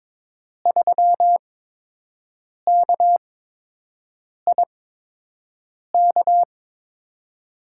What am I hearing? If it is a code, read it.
Morse code, heard as a keyed tone, about 22 wpm, spaced as for 6 wpm, 700 Hz. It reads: 3KIK
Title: USA Call Signs - 6 Characters - 8x Character Spacing - 22wpm